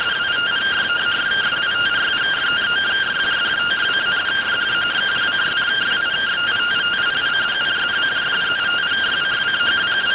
Two-tone teleprinter signal, about 170 Hz shift.